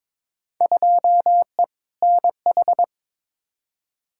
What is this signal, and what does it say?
Morse code, heard as a keyed tone, about 22 wpm, 700 Hz. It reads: 2E NH
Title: Field Day Exchanges - 22wpm